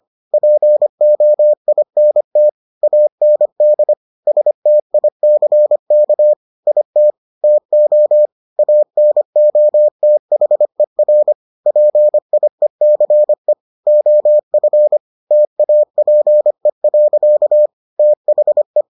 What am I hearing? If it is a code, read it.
Morse code, heard as a keyed tone, about 25 wpm, 600 Hz. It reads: POINT AND STICK IT TO ANOTHER PIECE OF TAPE. THE